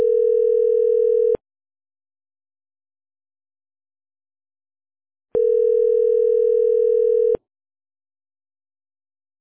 Ringback tone (North America), 440 + 480 Hz.